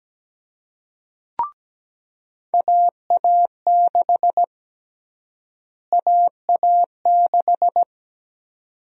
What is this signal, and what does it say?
Morse code, heard as a keyed tone, about 17 wpm, 700 Hz. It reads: AA6 AA6
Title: USA Call Sign Prefixes - Repeated 2x - 17wpm